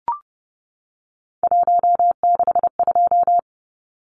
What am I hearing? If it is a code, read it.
Morse code, heard as a keyed tone, about 30 wpm, 700 Hz. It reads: 162
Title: Numbers - 3-Digit 30wpm